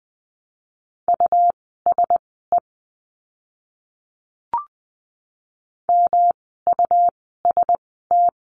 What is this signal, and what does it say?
Morse code, heard as a keyed tone, about 20 wpm, spaced as for 14 wpm, 700 Hz. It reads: USE MUST
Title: 237 Lesson 38f - Letter q - Words 20wpm